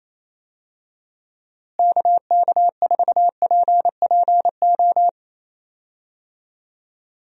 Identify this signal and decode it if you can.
Morse code, heard as a keyed tone, about 28 wpm, 700 Hz. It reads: KK4PPO
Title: USA Call Signs - 6 Characters - Repeated 2x - 28wpm